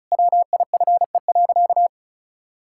Morse code, heard as a keyed tone, about 36 wpm, 700 Hz. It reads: WIFE.
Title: Pangram Challenge v2 - 35wpm